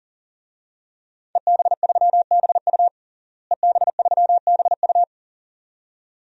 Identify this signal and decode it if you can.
Morse code, heard as a keyed tone, about 40 wpm, 700 Hz. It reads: EB3BU EB3BU
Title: DXCC-100 Call Signs - 5 Characters - Repeated 2x - 40wpm